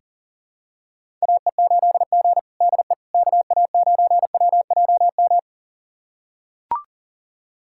Morse code, heard as a keyed tone, about 40 wpm, 700 Hz. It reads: AE8G DE KA9WJM
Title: USA Callsign DE Callsign 40wpm